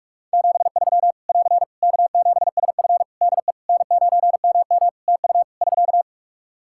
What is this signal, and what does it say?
Morse code, heard as a keyed tone, about 46 wpm, 700 Hz. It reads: 73 <AR> K7SF DE N9MM TU <SK>
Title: K1USN SST QSO Practice 45wpm